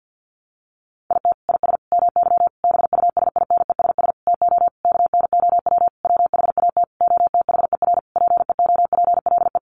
Morse code, heard as a keyed tone, about 50 wpm, 700 Hz. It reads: IT IS MY BUSINESS TO KNOW WHAT OTHER PEOPLE
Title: Sentences from Top 600 Words 50wpm